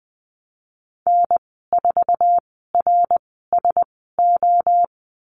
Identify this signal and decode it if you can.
Morse code, heard as a keyed tone, about 20 wpm, spaced as for 14 wpm, 700 Hz. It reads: N4RSO
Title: 069 Lesson 12g - Letter d - Call Signs 20wpm